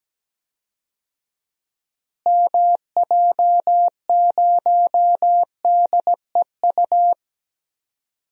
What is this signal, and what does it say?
Morse code, heard as a keyed tone, about 17 wpm, 700 Hz. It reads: MJ0DEU
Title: DXCC-100 Call Signs - 6 Characters - Repeated 2x - 17wpm